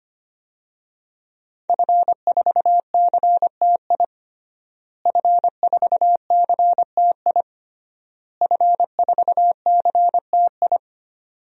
Morse code, heard as a keyed tone, about 25 wpm, 700 Hz. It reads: F4CTI F4CTI F4CTI
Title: DXCC-100 Call Signs - 5 Characters - Repeated 3x - 25wpm